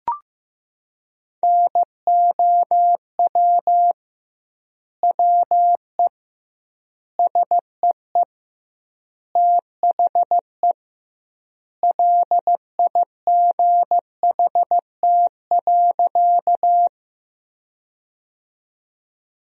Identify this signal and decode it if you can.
Morse code, heard as a keyed tone, about 16 wpm, 700 Hz. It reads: NOW WE SEE THE LIGHT.
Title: 2x Word-Spacing - Sentences from Top 200 Words - 15wpm